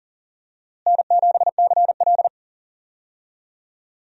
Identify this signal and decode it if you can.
Morse code, heard as a keyed tone, about 40 wpm, 700 Hz. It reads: N7CL
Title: USA Call Signs - 4 Characters - Repeated 3x - 40wpm